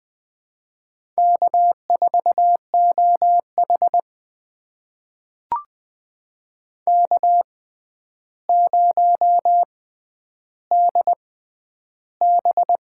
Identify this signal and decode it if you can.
Morse code, heard as a keyed tone, about 20 wpm, spaced as for 10 wpm, 700 Hz. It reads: K4OH K0DB
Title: USA Call Signs - 4 Characters - 6x Character Spacing - 20wpm